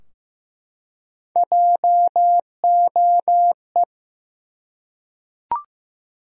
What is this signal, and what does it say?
Morse code, heard as a keyed tone, about 15 wpm, 700 Hz. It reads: JOE
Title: USA Common Names - Max 5 Letters - 15wpm